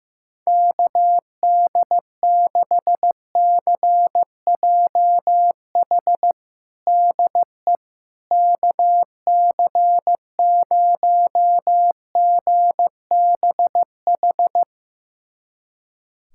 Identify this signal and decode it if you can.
Morse code, heard as a keyed tone, about 15 wpm, 700 Hz. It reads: KD6CJH DE KC0GBH
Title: USA Callsign DE Callsign 15wpm